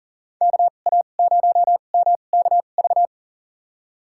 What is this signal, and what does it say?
Morse code, heard as a keyed tone, about 40 wpm, spaced as for 29 wpm, 700 Hz. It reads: KA0MKV
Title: USA Call Signs - 6 Characters - 2x Character Spacing - 40wpm